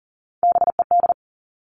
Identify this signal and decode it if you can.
Morse code, heard as a keyed tone, about 38 wpm, 700 Hz. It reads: BED
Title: Three Challenging Letter-Number Combinations 40wpm